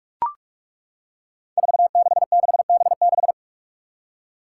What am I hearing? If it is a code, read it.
Morse code, heard as a keyed tone, about 41 wpm, 700 Hz. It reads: 466B6
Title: Five Challenging Letter-Number Combinations 45wpm